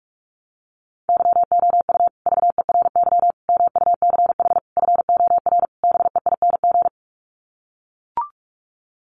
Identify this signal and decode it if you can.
Morse code, heard as a keyed tone, about 44 wpm, 700 Hz. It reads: YOU VERY MUCH FOR BEING
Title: Sets of 5 Words Encore 45wpm